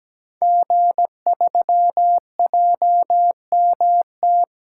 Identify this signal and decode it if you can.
Morse code, heard as a keyed tone, about 17 wpm, 700 Hz. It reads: G3JMT
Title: Code Groups - 5 Letter:Numbers - 17wpm